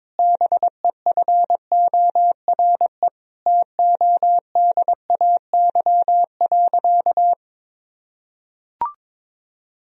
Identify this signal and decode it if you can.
Morse code, heard as a keyed tone, about 22 wpm, 700 Hz. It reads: BEFORE TODAY.
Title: Sentences from Top 1000 Words - Review Part 4 - 22wpm